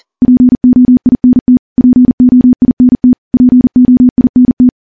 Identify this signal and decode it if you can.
Morse code, heard as a keyed tone, about 42 wpm, 250 Hz. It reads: POINT POINT POINT
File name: morse / morse-training-40WPM